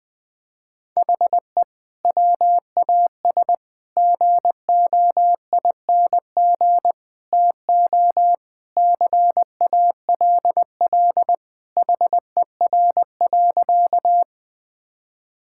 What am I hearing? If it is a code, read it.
Morse code, heard as a keyed tone, about 20 wpm, 700 Hz. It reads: HE WAS GOING TO CALL HER.
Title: Sentences with 4-Word Phrases - 20wpm